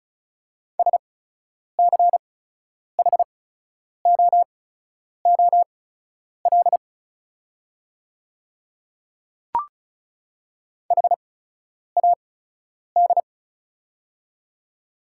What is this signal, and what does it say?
Morse code, heard as a keyed tone, about 34 wpm, spaced as for 10 wpm, 690 Hz. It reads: SCHOOL HAD
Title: 8x Character Spacing - Top 500 Words - 35wpm